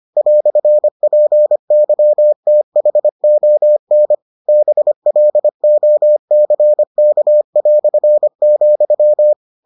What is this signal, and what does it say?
Morse code, heard as a keyed tone, about 25 wpm, 600 Hz. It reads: "PYTHON BLOCK",